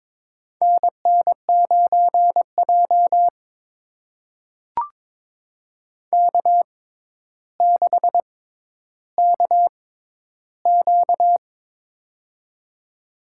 Morse code, heard as a keyed tone, about 22 wpm, spaced as for 11 wpm, 700 Hz. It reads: NN9J K6KQ
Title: USA Call Signs - 4 Characters - 6x Character Spacing - 22wpm